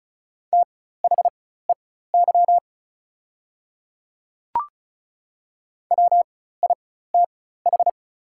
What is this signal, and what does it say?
Morse code, heard as a keyed tone, about 36 wpm, spaced as for 16 wpm, 700 Hz. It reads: THEY WITH
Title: 4x Character Spacing - Top 100 Words - 35wpm